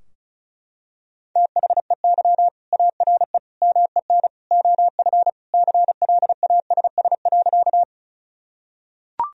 Morse code, heard as a keyed tone, about 36 wpm, 700 Hz. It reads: THEY ARE MEN OF CLASS.
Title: Sentences from Top 1000 Words - Review Part 2 - 35wpm